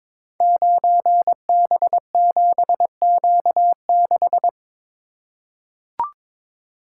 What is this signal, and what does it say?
Morse code, heard as a keyed tone, about 22 wpm, 700 Hz. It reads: 9B7Q6
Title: Code Groups - 5 Letter:Numbers - 22wpm